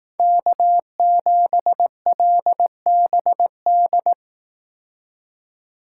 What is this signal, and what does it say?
Morse code, heard as a keyed tone, about 18 wpm, 700 Hz. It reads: K7LBD
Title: Speed Racing - USA 1x3 Call Signs 15wpm